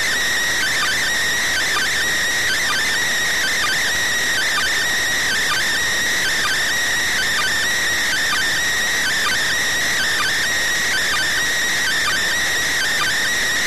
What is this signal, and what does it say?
Two-tone teleprinter signal, about 400 Hz shift.